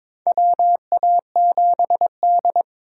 Morse code, heard as a keyed tone, about 21 wpm, 700 Hz. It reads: WA7D
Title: USA Call Signs - 3x Character Spacing - 22wpm